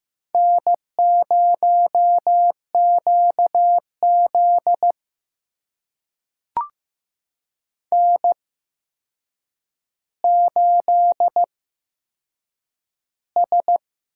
Morse code, heard as a keyed tone, about 15 wpm, spaced as for 7 wpm, 700 Hz. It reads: N0QZ N8S
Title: USA Call Signs - 4 Characters - 8x Character Spacing - 15wpm